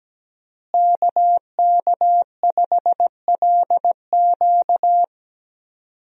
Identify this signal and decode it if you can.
Morse code, heard as a keyed tone, about 17 wpm, 700 Hz. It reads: KK5LQ
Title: USA Call Signs - 5 Characters - Repeated 2x - 17wpm